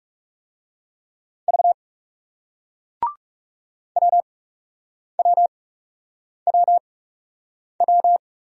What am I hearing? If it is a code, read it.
Morse code, heard as a keyed tone, about 41 wpm, spaced as for 8 wpm, 700 Hz. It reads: V WWWW